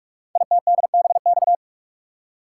Morse code, heard as a keyed tone, about 44 wpm, 700 Hz. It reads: ITDBX
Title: Code Groups - 5 Letters - 45wpm